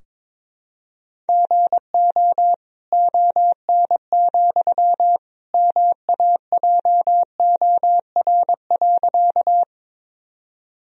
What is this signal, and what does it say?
Morse code, heard as a keyed tone, about 22 wpm, 700 Hz. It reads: GO ON, MAJOR.
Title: Sentences from Top 1000 Words - Review Part 4 - 22wpm